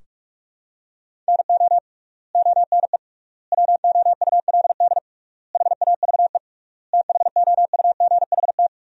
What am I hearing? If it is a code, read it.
Morse code, heard as a keyed tone, about 46 wpm, 700 Hz. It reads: NO ONE WOULD HAVE THOUGHT
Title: 3x Word-Spacing - Sentences from Top 300 Words - 45wpm